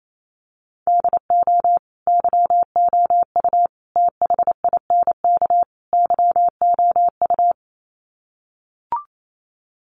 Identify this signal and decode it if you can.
Morse code, heard as a keyed tone, about 28 wpm, 700 Hz. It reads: DO YOU THINK YOU
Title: Sets of 4 Words 28wpm